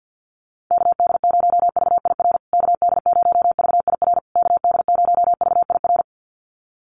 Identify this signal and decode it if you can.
Morse code, heard as a keyed tone, about 50 wpm, 700 Hz. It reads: KD0VIR KD0VIR KD0VIR
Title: USA Call Signs Repeated 3x 50wpm